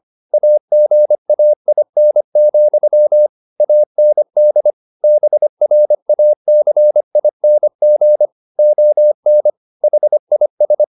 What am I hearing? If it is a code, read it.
Morse code, heard as a keyed tone, about 25 wpm, 600 Hz. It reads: AGAIN, AND BRACING ON HIS
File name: the old man and the sea part 1 25wpm